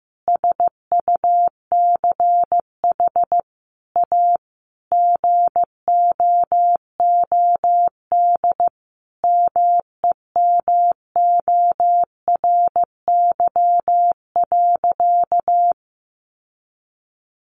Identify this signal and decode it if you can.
Morse code, heard as a keyed tone, about 15 wpm, 700 Hz. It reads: SUCH A GOOD MEMORY.
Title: Sentences from Top 1300 Words 15wpm